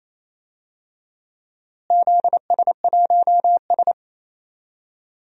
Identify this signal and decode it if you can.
Morse code, heard as a keyed tone, about 28 wpm, 700 Hz. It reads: ZS1S
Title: DXCC-100 Call Signs - 4 Characters - Repeated 2x - 28wpm